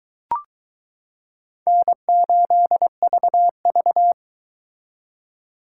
Morse code, heard as a keyed tone, about 23 wpm, 700 Hz. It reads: N8VV